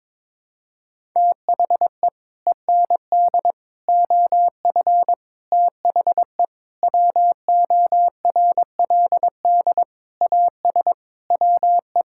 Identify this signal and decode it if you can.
Morse code, heard as a keyed tone, about 22 wpm, 700 Hz. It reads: THE END OF THE WORLD AS WE
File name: Sets of 9 Words - 22wpm